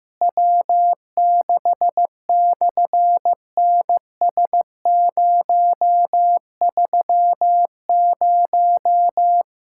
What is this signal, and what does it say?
Morse code, heard as a keyed tone, about 15 wpm, 700 Hz. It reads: W6/NS030
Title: SOTA QSO Practice 15wpm